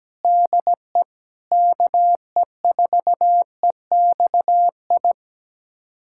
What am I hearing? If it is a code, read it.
Morse code, heard as a keyed tone, about 17 wpm, 700 Hz. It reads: DE KE4EXI
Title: USA Callsign DE Callsign 17wpm